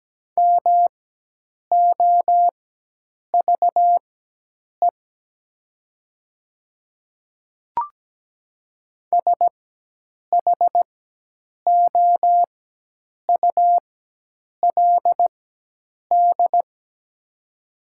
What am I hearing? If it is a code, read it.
Morse code, heard as a keyed tone, about 17 wpm, spaced as for 8 wpm, 700 Hz. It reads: MOVE SHOULD
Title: 4x Character Spacing - Top 300 Words - 17wpm